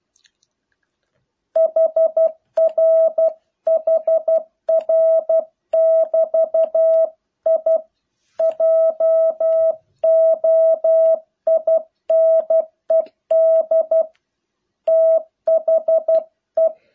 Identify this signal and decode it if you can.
Morse code, heard as a keyed tone, about 12 wpm, 650 Hz. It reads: HR HR <BT> I JOINED THE